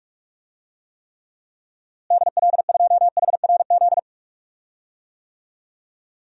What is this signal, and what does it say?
Morse code, heard as a keyed tone, about 41 wpm, 700 Hz. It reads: DL2HRZ
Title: DXCC-100 Call Signs - 6 Characters - Repeated 3x - 45wpm